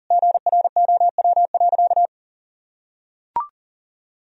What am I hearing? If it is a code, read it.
Morse code, heard as a keyed tone, about 40 wpm, 700 Hz. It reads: GROW.
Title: Sentences from Top 1000 Words - Review Part 1 - 40wpm